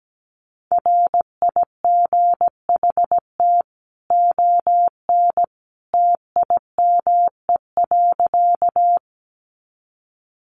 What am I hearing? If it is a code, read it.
Morse code, heard as a keyed tone, about 17 wpm, 700 Hz. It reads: RIGHT ON TIME.
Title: A Sound Of Thunder 17wpm